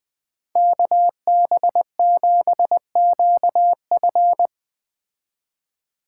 Morse code, heard as a keyed tone, about 20 wpm, 700 Hz. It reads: KB7QF